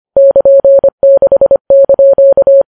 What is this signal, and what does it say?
Morse code, heard as a keyed tone, about 25 wpm, 550 Hz. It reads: <KN>6)
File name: Files Web tls / Morse on9fv / morse25kps